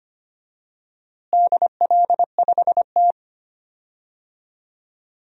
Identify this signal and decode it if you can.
Morse code, heard as a keyed tone, about 25 wpm, 700 Hz. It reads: DL5T
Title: DXCC-100 Call Signs - 4 Characters - Repeated 3x - 25wpm